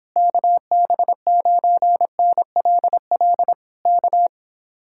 Morse code, heard as a keyed tone, about 26 wpm, 700 Hz. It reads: KB9NLL K